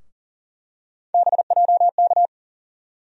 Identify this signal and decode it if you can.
Morse code, heard as a keyed tone, about 40 wpm, 700 Hz. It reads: BJK
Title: USA Call Sign Postfixes - 3 Letters 40wpm